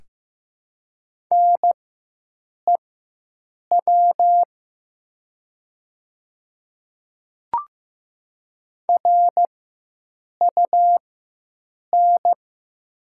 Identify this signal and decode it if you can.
Morse code, heard as a keyed tone, about 15 wpm, spaced as for 7 wpm, 700 Hz. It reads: NEW RUN
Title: 4x Character Spacing - Top 300 Words - 15wpm